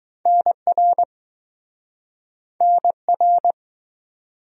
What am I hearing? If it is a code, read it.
Morse code, heard as a keyed tone, about 20 wpm, 700 Hz. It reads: NR NR